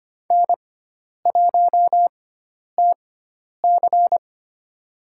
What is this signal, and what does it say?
Morse code, heard as a keyed tone, about 25 wpm, spaced as for 10 wpm, 700 Hz. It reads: N1TC
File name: Lesson 16g - Letter c - Call Signs 25wpm